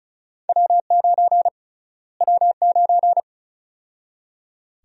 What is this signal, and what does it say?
Morse code, heard as a keyed tone, about 34 wpm, 700 Hz. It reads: W9 W9
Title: USA Call Sign Prefixes - Repeated 2x - 35wpm